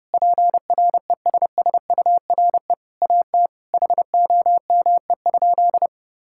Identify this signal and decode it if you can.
Morse code, heard as a keyed tone, about 32 wpm, 700 Hz. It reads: PRESSURE AT HOME?
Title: Sentences from Top 1000 Words - Review Part 6 - 30wpm